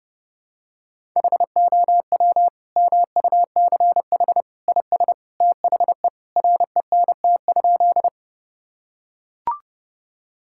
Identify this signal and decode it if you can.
Morse code, heard as a keyed tone, about 30 wpm, 700 Hz. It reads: HOW MUCH IS THE RENT?
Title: Sentences from Top 1000 Words - Review Part 5 - 30wpm